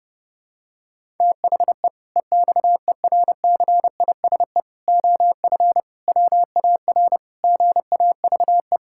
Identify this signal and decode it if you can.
Morse code, heard as a keyed tone, about 30 wpm, 700 Hz. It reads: THE EXERCISE OF WAR GAVE
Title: Sentences from Top 1000 Words - Review Part 6 - 30wpm